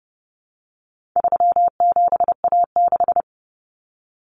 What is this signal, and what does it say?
Morse code, heard as a keyed tone, about 30 wpm, 700 Hz. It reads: 37A6
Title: Common Cut Numbers - 4-Digit 30wpm